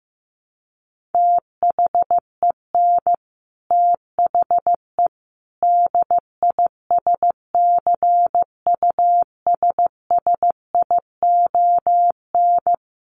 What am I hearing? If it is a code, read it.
Morse code, heard as a keyed tone, about 15 wpm, 700 Hz. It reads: THEN THE DISCUSSION
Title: Sentences from Top 1100 Words 15wpm